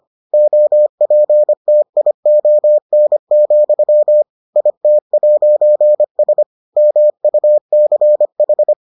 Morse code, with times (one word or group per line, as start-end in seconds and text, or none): 0.3-4.2 OPTION,
4.6-6.4 IT'S
6.8-8.7 MUCH